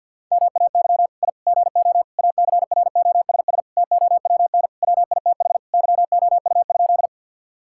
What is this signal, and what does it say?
Morse code, heard as a keyed tone, about 50 wpm, 700 Hz. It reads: MAY I GO ACROSS TOWN WITH YOU?
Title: Sentences from Top 1000 Words - Review Part 5 - 50wpm